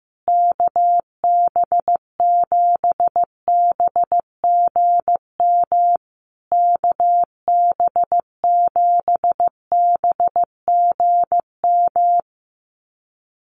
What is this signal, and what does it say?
Morse code, heard as a keyed tone, about 15 wpm, 700 Hz. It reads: KB7BGM KB7BGM
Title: USA Call Signs Repeated 3x 15wpm